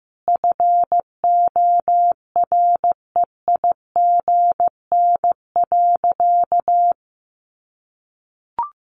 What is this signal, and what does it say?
Morse code, heard as a keyed tone, about 14 wpm, 700 Hz. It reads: FOREIGN.
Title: Sentences from Top 1400 Words 15wpm